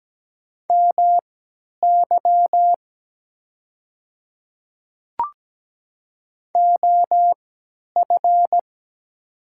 Morse code, heard as a keyed tone, about 17 wpm, spaced as for 10 wpm, 690 Hz. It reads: MY OF
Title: 3x Character Spacing - Top 100 Words - 17wpm